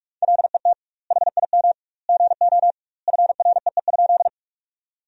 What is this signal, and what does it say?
Morse code, heard as a keyed tone, about 46 wpm, 700 Hz. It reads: LET HIM GO FREE?
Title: 2x Word-Spacing - Sentences from Top 500 Words - 45wpm